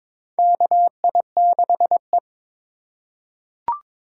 Morse code, heard as a keyed tone, about 21 wpm, 700 Hz. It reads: KI6E
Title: USA Call Signs - 4 Characters - 4x Character Spacing - 22wpm